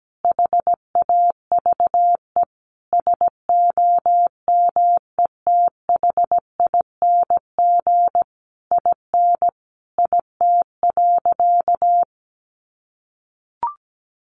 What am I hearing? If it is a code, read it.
Morse code, heard as a keyed tone, about 17 wpm, 700 Hz. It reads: HAVE SOMETHING IN IT.
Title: Sentences from Top 1100 Words 17wpm